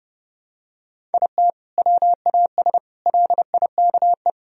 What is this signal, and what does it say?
Morse code, heard as a keyed tone, about 30 wpm, 700 Hz. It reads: IT WAS LIKE
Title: Sentences from Top 1000 Words - Review Part 1 - 30wpm